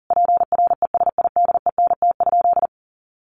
Morse code, heard as a keyed tone, about 40 wpm, 700 Hz. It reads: PRESIDENT?
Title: Sentences from Top 1300 Words 40wpm